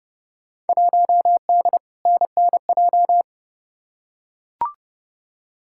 Morse code, heard as a keyed tone, about 30 wpm, 700 Hz. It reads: 1D NNJ
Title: Field Day Exchanges Encore - 30wpm